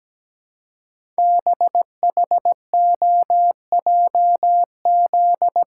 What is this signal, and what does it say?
Morse code, heard as a keyed tone, about 17 wpm, 700 Hz. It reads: BHOJZ
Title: Code Groups - 5 Letters - 17wpm